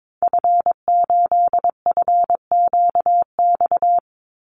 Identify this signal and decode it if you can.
Morse code, heard as a keyed tone, about 22 wpm, 700 Hz. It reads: F8FQX
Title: Call Signs Repeated 3x - 22wpm